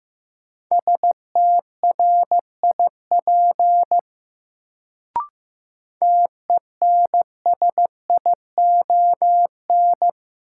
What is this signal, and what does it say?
Morse code, heard as a keyed tone, about 15 wpm, 700 Hz. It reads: STRIP TENSION
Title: Sentences from Top 1600 Words - New Word Drill Alt - 15wpm